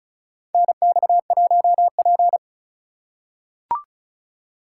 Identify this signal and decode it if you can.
Morse code, heard as a keyed tone, about 36 wpm, 700 Hz. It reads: NX1P
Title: USA Call Signs - 4 Characters - 4x Character Spacing - 35wpm